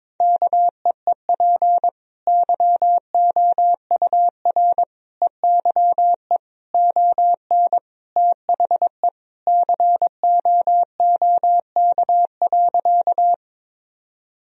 Morse code, heard as a keyed tone, about 22 wpm, 700 Hz. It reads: KEEP YOUR EYE ON THE COOK.
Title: Sentences from Top 1000 Words - Review Part 4 - 22wpm